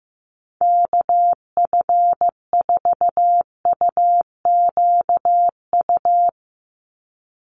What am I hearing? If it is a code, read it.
Morse code, heard as a keyed tone, about 15 wpm, 700 Hz. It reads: KF4UQU
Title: USA Call Signs 15wpm